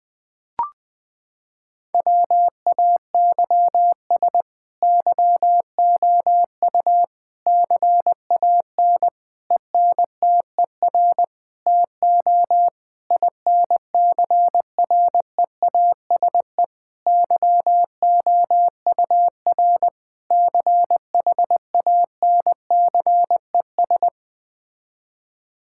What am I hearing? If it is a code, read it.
Morse code, heard as a keyed tone, about 20 wpm, 700 Hz. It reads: WAYS YOU CAN ENTER TO INCREASE YOUR CHANCES
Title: Sets of 8 Words - 20wpm